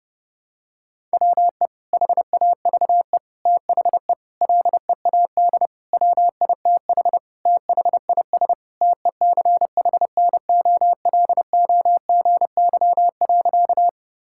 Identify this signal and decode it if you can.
Morse code, heard as a keyed tone, about 32 wpm, 700 Hz. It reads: WE HAVE THE LEAD WITH THIS TECHNOLOGY.
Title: Sentences from Top 1000 Words - Review Part 5 - 30wpm